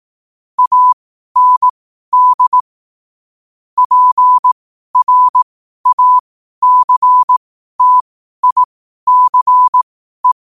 Morse code, heard as a keyed tone, about 18 wpm, spaced as for 12 wpm, 1000 Hz. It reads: AND PRACTICE
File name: qotdfiles / 12wpm